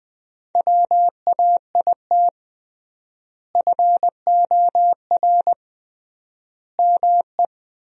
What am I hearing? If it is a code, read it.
Morse code, heard as a keyed tone, about 20 wpm, 700 Hz. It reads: WAIT FOR ME
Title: 3x Word-Spacing - Sentences from Top 500 Words - 20wpm